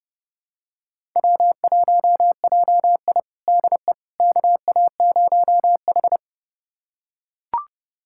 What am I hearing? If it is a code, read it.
Morse code, heard as a keyed tone, about 30 wpm, 700 Hz. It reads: W1JI DE KA0H
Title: USA Callsign DE Callsign - 4 Characters - 30wpm